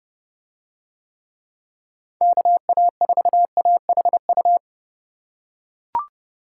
Morse code, heard as a keyed tone, about 29 wpm, 700 Hz. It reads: KA4AHU